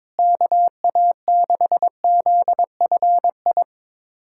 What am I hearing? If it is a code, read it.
Morse code, heard as a keyed tone, about 22 wpm, 700 Hz. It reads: KA6ZFI
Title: USA Callsign DE Callsign - 6 Characters - 22wpm